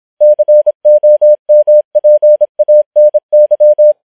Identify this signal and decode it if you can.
Morse code, heard as a keyed tone, about 26 wpm, 600 Hz. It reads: COMPANY